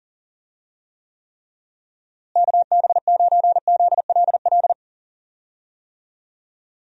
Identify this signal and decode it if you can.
Morse code, heard as a keyed tone, about 39 wpm, 700 Hz. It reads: KB9ZLL
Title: USA Call Signs - 6 Characters - Repeated 3x - 40wpm